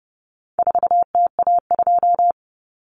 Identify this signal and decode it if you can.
Morse code, heard as a keyed tone, about 28 wpm, 700 Hz. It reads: 4TA2
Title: Common Cut Numbers - 4-Digit 30wpm